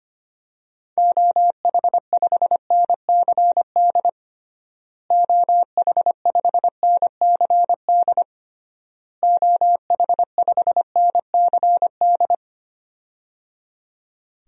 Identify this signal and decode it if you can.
Morse code, heard as a keyed tone, about 25 wpm, 700 Hz. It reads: OH5NCD OH5NCD OH5NCD
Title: DXCC-100 Call Signs - 6 Characters - Repeated 3x - 25wpm